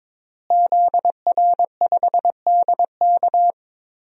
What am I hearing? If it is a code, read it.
Morse code, heard as a keyed tone, about 22 wpm, 700 Hz. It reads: ZR5DK
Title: DXCC-100 Call Signs - 5 Characters - Repeated 2x - 22wpm